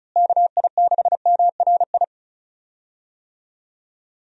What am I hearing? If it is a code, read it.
Morse code, heard as a keyed tone, about 34 wpm, 700 Hz. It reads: KI6MRI
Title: USA Call Signs - 6 Characters - Repeated 3x - 35wpm